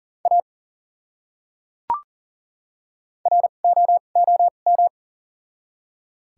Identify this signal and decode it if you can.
Morse code, heard as a keyed tone, about 38 wpm, spaced as for 28 wpm, 700 Hz. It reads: A ROOM
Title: 2x Character Spacing - Top 500 Words - 40wpm